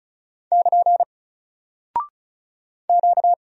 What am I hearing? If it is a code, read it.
Morse code, heard as a keyed tone, about 36 wpm, 700 Hz. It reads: <KN> Q
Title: Single Character (Letter:Number:Punctuation:Prosign) 35wpm